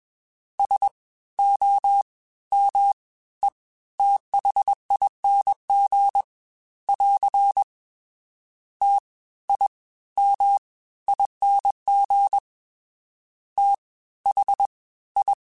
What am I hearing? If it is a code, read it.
Morse code, heard as a keyed tone, about 21 wpm, spaced as for 15 wpm, 800 Hz. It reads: SOMETHING <AR> TIMING THI